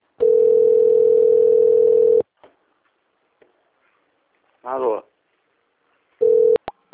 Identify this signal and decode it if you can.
Ringback tone (North America), 440 + 480 Hz, with speech.
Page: Cosmopolis - SP